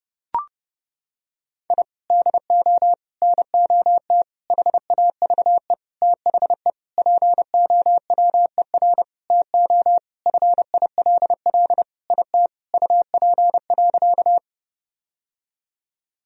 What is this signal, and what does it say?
Morse code, heard as a keyed tone, about 32 wpm, 700 Hz. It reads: I DO NOT HAVE THE POWER TO FILL IT UP.
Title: Sentences from Top 1000 Words - Review Part 3 - 30wpm